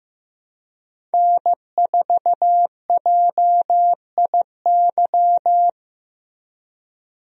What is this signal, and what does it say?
Morse code, heard as a keyed tone, about 15 wpm, 700 Hz. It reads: N4JIY
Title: USA Call Signs - 5 Characters - Repeated 3x - 15wpm